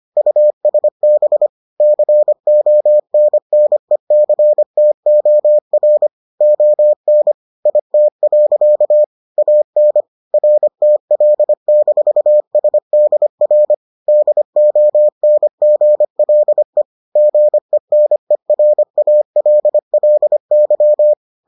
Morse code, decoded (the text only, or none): USB CONNECTOR ON IT. AN RTL-SDR DONGLE GENERALLY